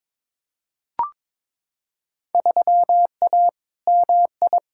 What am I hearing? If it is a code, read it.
Morse code, heard as a keyed tone, about 22 wpm, 700 Hz. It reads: 3A MI
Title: Field Day Exchanges - 22wpm